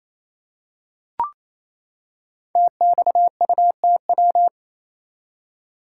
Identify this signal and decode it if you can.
Morse code, heard as a keyed tone, about 27 wpm, 690 Hz. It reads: TXUTW
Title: Code Groups - 5 Letters - 28wpm